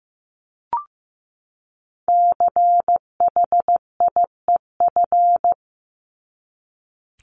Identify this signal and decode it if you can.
Morse code, heard as a keyed tone, about 15 wpm, 700 Hz. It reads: CHIEF
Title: LY Suffix - Top 100 Words with Base Words 15wpm